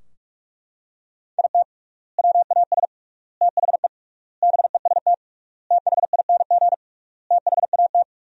Morse code, heard as a keyed tone, about 41 wpm, 700 Hz. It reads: IT WAS THE BEST THING THAT
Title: 3x Word-Spacing - Sentences from Top 400 Words - 45wpm